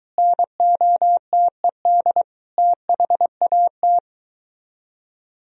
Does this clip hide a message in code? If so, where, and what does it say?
Yes: Morse code from 0.2 to 4.0 s, heard as a keyed tone, about 23 wpm, 700 Hz. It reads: NOTED THAT